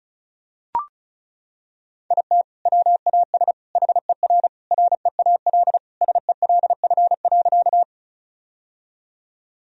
Morse code, heard as a keyed tone, about 36 wpm, 700 Hz. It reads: IT WAS HER REAL SELF.
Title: Sentences from Top 1000 Words - Review Part 1 - 35wpm